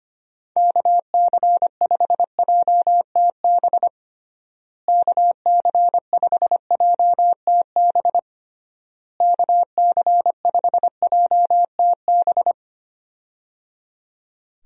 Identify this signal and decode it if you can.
Morse code, heard as a keyed tone, about 25 wpm, 700 Hz. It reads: KC5JTB KC5JTB KC5JTB
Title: USA Call Signs - 6 Characters - Repeated 3x - 25wpm